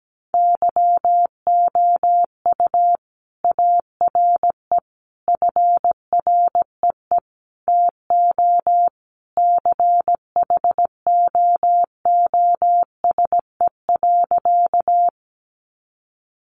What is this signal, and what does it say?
Morse code, heard as a keyed tone, about 17 wpm, 700 Hz. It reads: YOU ARE FREE TO CHOOSE.
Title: Sentences from Top 700 Words 17wpm